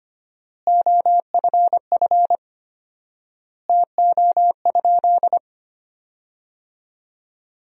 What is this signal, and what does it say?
Morse code, heard as a keyed tone, about 25 wpm, 700 Hz. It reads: OFF TO?
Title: 4x Word-Spacing - Sentences from Top 300 Words - 25wpm